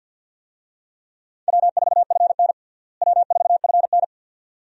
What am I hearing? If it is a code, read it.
Morse code, heard as a keyed tone, about 50 wpm, 700 Hz. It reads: W4FN W4FN
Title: USA Call Signs - 4 Characters - Repeated 2x - 50wpm